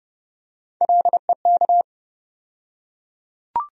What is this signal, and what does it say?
Morse code, heard as a keyed tone, about 32 wpm, 700 Hz. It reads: LEK
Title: USA Call Sign Postfixes - 3 Letters 30wpm